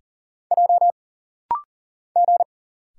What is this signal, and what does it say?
Morse code, heard as a keyed tone, about 41 wpm, 700 Hz. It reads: J G
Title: Single Letters - Rapid-Fire 40wpm